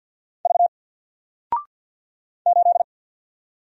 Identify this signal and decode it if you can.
Morse code, heard as a keyed tone, about 54 wpm, 700 Hz. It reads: V 8
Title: Single Character (Letter:Number:Punctuation:Prosign) 50wpm